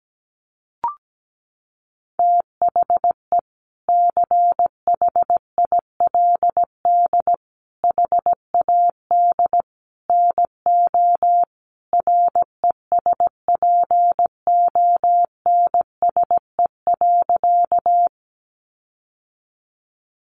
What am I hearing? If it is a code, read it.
Morse code, heard as a keyed tone, about 17 wpm, 700 Hz. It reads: THE CHILD HAD NO RESPONSE.
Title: Sentences from Top 1400 Words 17wpm